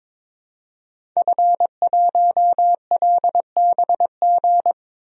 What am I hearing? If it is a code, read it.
Morse code, heard as a keyed tone, about 22 wpm, 700 Hz. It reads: F1LBG
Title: DXCC-100 Call Signs - 5 Characters - 22wpm